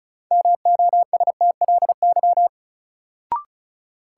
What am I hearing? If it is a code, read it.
Morse code, heard as a keyed tone, about 34 wpm, 700 Hz. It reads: MOSTLY
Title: Sentences from Top 1300 Words - New Word Drill Alt - 35wpm